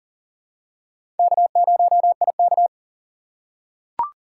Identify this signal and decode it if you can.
Morse code, heard as a keyed tone, about 40 wpm, 700 Hz. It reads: K0IK
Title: USA Call Signs - 4 Characters - 6x Character Spacing - 40wpm